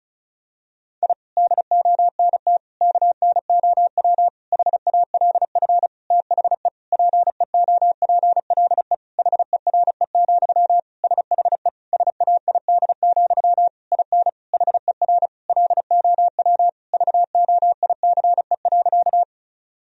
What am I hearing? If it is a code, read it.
Morse code, heard as a keyed tone, about 34 wpm, 700 Hz. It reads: I DONT KNOW HALF THE PEOPLE HERE, SHE SAID, IN HER LOW VOICE.
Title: Sentences from Top 1000 Words - Review Part 1 - 35wpm